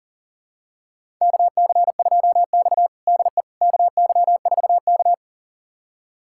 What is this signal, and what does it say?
Morse code, heard as a keyed tone, about 40 wpm, 700 Hz. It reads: KC2X DE KY4K
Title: USA Callsign DE Callsign - 4 Characters - 40wpm